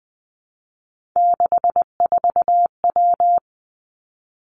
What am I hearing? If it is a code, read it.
Morse code, heard as a keyed tone, about 20 wpm, 700 Hz. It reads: 64W
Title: Three Challenging Letter-Number Combinations 20wpm